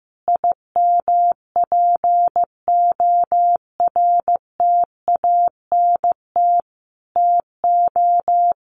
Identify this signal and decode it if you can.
Morse code, heard as a keyed tone, about 15 wpm, 700 Hz. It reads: IMPORTANT TO
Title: Sentences from Top 1300 Words 15wpm